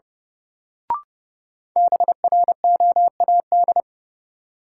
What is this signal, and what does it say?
Morse code, heard as a keyed tone, about 31 wpm, 700 Hz. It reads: BROAD
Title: Sentences from Top 1000 Words - New Word Drill - 30wpm